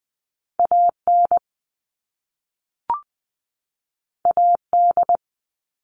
Morse code, heard as a keyed tone, about 20 wpm, 700 Hz. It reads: AN AD
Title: All Cut Numbers - 2-Digit 20wpm